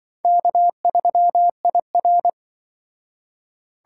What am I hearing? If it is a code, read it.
Morse code, heard as a keyed tone, about 25 wpm, 700 Hz. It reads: K3IR
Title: Speed Racing - USA 1x2 Call Signs 20wpm